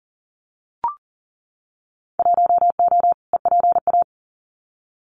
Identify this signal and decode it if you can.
Morse code, heard as a keyed tone, about 40 wpm, 700 Hz. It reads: 1O EPA
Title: Winter Field Day Exchanges 40wpm